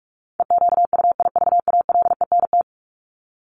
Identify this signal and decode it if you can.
Morse code, heard as a keyed tone, about 44 wpm, 700 Hz. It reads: EQUIVALENT
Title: Top 1501-1750 Words 45wpm